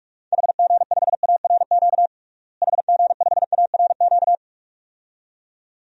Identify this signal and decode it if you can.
Morse code, heard as a keyed tone, about 46 wpm, 700 Hz. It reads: HG5ARQ HG5ARQ
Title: DXCC-100 Call Signs - 6 Characters - Repeated 2x - 45wpm